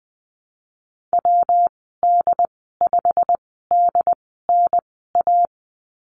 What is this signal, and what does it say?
Morse code, heard as a keyed tone, about 20 wpm, spaced as for 15 wpm, 700 Hz. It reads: WD5DNA
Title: 122 Lesson 20g - Number 3 - Call Signs 20wpm